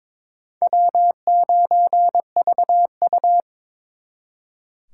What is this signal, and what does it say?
Morse code, heard as a keyed tone, about 26 wpm, 700 Hz. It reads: W9VU
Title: Speed Racing - USA 1x2 Call Signs 22wpm